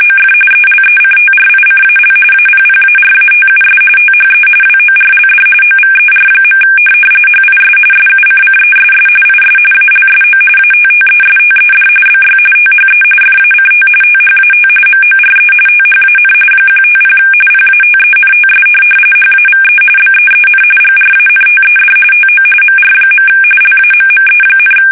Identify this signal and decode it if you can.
RTTY, about 45.45 baud. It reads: OF ELECTRICAL APPARATUS THE WORLD OVER. HIS SUCCESS IS ALSO ATTESTED BY THE FACT THAT MORE THAN 150 PATENTS HAVE BEEN GRANTED TO MR. LAMME,
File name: fsk850 / benjamin-g-lamme